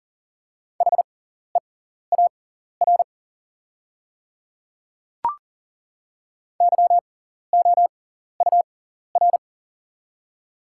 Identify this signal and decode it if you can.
Morse code, heard as a keyed tone, about 40 wpm, spaced as for 14 wpm, 690 Hz. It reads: HEAR YOUR
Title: 6x Character Spacing - Top 300 Words - 40wpm